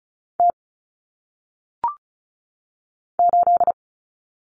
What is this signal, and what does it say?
Morse code, heard as a keyed tone, about 36 wpm, 700 Hz. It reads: T 8
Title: Common Cut Numbers - 1-Digit 35wpm